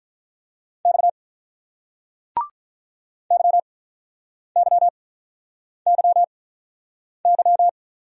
Morse code, heard as a keyed tone, about 48 wpm, spaced as for 9 wpm, 700 Hz. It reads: X YYYY